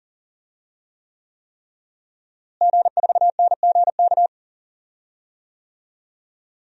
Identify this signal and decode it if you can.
Morse code, heard as a keyed tone, about 40 wpm, 700 Hz. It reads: G4NGK
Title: DXCC-100 Call Signs - 5 Characters - Repeated 3x - 40wpm